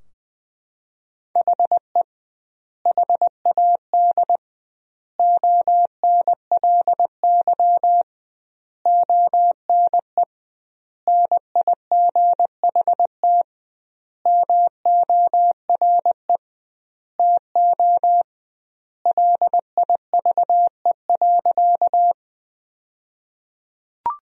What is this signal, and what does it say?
Morse code, heard as a keyed tone, about 21 wpm, 700 Hz. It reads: HE HAD ONLY ONE NIGHT MORE TO LIVE.
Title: 2x Word-Spacing - Sentences from Top 300 Words - 20wpm